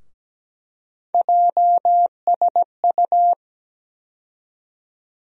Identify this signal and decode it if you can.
Morse code, heard as a keyed tone, about 17 wpm, 700 Hz. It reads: JSU
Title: USA Call Sign Postfixes - 3 Letters 17wpm